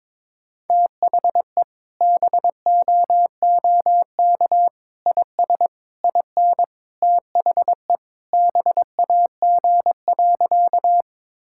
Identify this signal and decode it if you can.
Morse code, heard as a keyed tone, about 22 wpm, 700 Hz. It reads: THE BOOK IS IN THE BAG.
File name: practice / 22wpm / Sentences with 3-Word Phrases - 22wpm